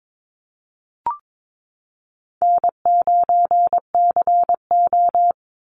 Morse code, heard as a keyed tone, about 22 wpm, 700 Hz. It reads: N9CO
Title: Easy Call Signs Repeated 2x - 22wpm